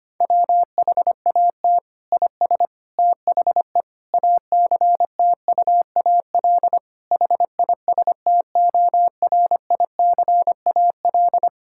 Morse code, heard as a keyed tone, about 25 wpm, 700 Hz. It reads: WHAT IS THE ACTUAL HISTORICAL
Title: Sentences from Top 1000 Words - Review Part 5 - 25wpm